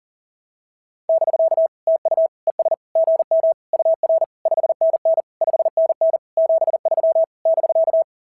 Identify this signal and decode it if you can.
Morse code, heard as a keyed tone, about 34 wpm, 660 Hz. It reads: <BK> TU ES GM UR 5NN 5NN 73 <BK>
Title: SOTA QSO Practice 40wpm